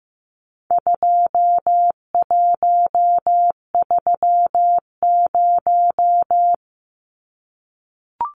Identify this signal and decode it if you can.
Morse code, heard as a keyed tone, about 15 wpm, 700 Hz. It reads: 2130
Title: Numbers - 4-Digit 15wpm